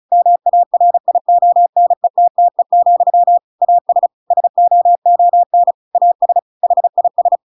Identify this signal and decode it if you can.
Morse code, heard as a keyed tone, about 34 wpm, 700 Hz. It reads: MARIONETTE, AS SOON AS HIS
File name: Pinocchio / Pinocchio ch8 35 35 ik1hge